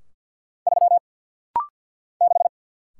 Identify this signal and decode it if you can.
Morse code, heard as a keyed tone, about 48 wpm, 690 Hz. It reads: 3 6
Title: Single Letter-Number - Rapid-Fire 50wpm